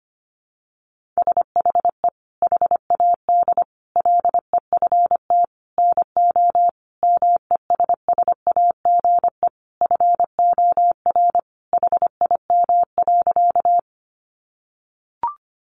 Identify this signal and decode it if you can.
Morse code, heard as a keyed tone, about 25 wpm, 700 Hz. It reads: SHE HAD LEFT NO MESSAGE FOR HIM.
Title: Sentences from Top 1100 Words 25wpm